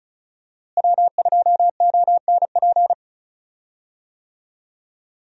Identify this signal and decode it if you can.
Morse code, heard as a keyed tone, about 36 wpm, 700 Hz. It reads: W2ONP
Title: USA Call Signs - 5 Characters - Repeated 3x - 35wpm